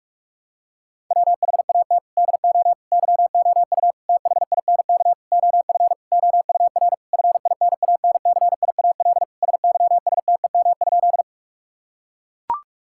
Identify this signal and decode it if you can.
Morse code, heard as a keyed tone, about 42 wpm, 700 Hz. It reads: WHAT DO YOU THINK OF OUR FINANCIAL SYSTEM?
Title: Sentences from Top 1000 Words - Review Part 5 - 45wpm